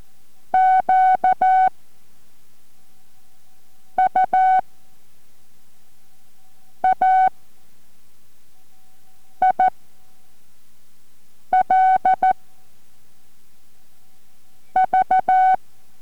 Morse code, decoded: QUAIL V